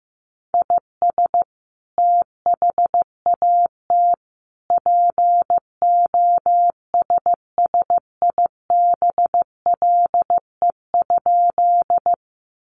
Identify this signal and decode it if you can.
Morse code, heard as a keyed tone, about 14 wpm, 700 Hz. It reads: IS THAT POSSIBLE?
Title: Sentences from Top 500 Words 15wpm